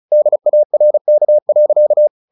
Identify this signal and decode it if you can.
Morse code, heard as a keyed tone, about 34 wpm, 600 Hz. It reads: DARK.
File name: the old man and the sea / the old man and the sea 1 35wpm